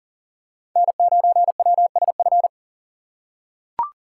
Morse code, heard as a keyed tone, about 40 wpm, 700 Hz. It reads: N9WSF
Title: USA Call Signs - 5 Characters - 3x Character Spacing - 40wpm